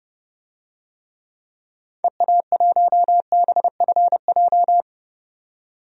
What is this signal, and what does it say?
Morse code, heard as a keyed tone, about 30 wpm, 700 Hz. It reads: EA1BFJ
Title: DXCC-100 Call Signs - 6 Characters - 30wpm